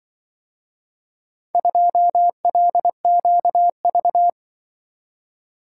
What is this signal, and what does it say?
Morse code, heard as a keyed tone, about 24 wpm, 700 Hz. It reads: 2LQV
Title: Speed-Racing - Four Letter-Number Combinations - 20wpm